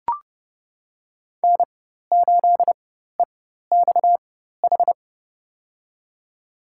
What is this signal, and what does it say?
Morse code, heard as a keyed tone, about 30 wpm, spaced as for 14 wpm, 700 Hz. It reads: N8EXH
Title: USA Call Signs - 4x Character Spacing - 30wpm